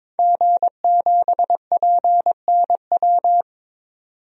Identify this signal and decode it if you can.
Morse code, heard as a keyed tone, about 22 wpm, 700 Hz. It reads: G7PNW
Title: DXCC-100 Call Signs - 5 Characters - 22wpm